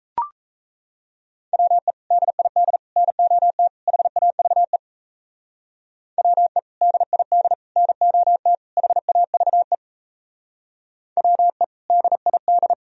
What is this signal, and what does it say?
Morse code, heard as a keyed tone, about 34 wpm, 700 Hz. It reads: WE DID NOT HAVE WE DID NOT HAVE WE DID
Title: Speed-Racing - Sets of 4 Words 28wpm